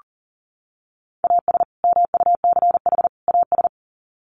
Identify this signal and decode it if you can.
Morse code, heard as a keyed tone, about 40 wpm, 700 Hz. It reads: AS MUCH AS
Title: Sets of 3 Words 40wpm